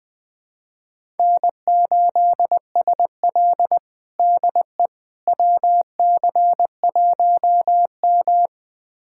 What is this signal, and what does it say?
Morse code, heard as a keyed tone, about 21 wpm, 700 Hz. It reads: N8SL DE WC1M
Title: USA Callsign DE Callsign - 4 Characters - 20wpm